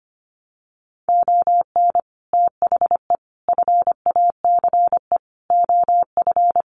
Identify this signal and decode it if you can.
Morse code, heard as a keyed tone, about 25 wpm, 700 Hz. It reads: ON THE FACE OF
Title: Sets of 5 Words Encore2 25wpm